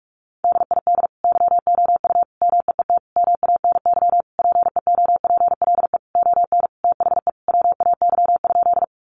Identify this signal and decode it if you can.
Morse code, heard as a keyed tone, about 44 wpm, 700 Hz. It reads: DID YOU MEET MANY PEOPLE ON THE WAY?
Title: Sentences from Top 1100 Words 45wpm